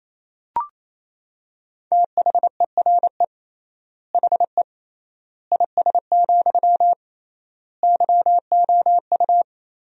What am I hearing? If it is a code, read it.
Morse code, heard as a keyed tone, about 28 wpm, 700 Hz. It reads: THERE HE IS, YOU
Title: 3x Word-Spacing - Sentences from Top 100 Words - 28wpm